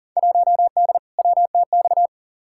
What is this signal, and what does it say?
Morse code, heard as a keyed tone, about 40 wpm, 700 Hz. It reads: 1D WTX
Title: Field Day Exchanges - 40wpm